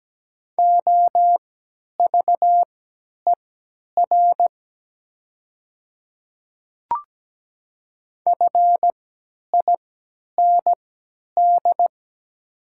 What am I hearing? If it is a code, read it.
Morse code, heard as a keyed tone, about 17 wpm, spaced as for 10 wpm, 700 Hz. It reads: OVER FIND
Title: 3x Character Spacing - Top 100 Words - 17wpm